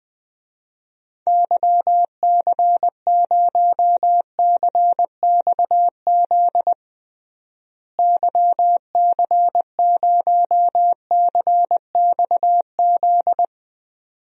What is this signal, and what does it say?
Morse code, heard as a keyed tone, about 20 wpm, 700 Hz. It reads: YC0CXZ YC0CXZ
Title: DXCC-100 Call Signs - 6 Characters - Repeated 2x - 20wpm